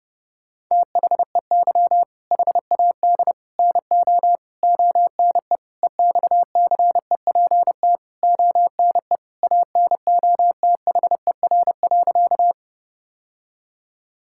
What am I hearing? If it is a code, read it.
Morse code, heard as a keyed tone, about 30 wpm, 700 Hz. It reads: THEY HAD NO ONE EXCEPT ONE ANOTHER.
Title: Sentences from Top 1600 Words 30wpm